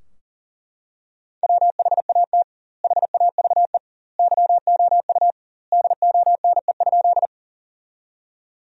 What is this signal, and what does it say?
Morse code, heard as a keyed tone, about 40 wpm, 700 Hz. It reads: WHAT HAVE YOU DONE?
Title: 2x Word-Spacing - Sentences from Top 500 Words - 40wpm